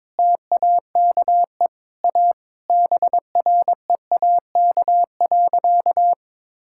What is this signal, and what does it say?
Morse code, heard as a keyed tone, about 22 wpm, 700 Hz. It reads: TAKE A BREAK.
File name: practice / 22wpm / Sentences with 4-Word Phrases - 22wpm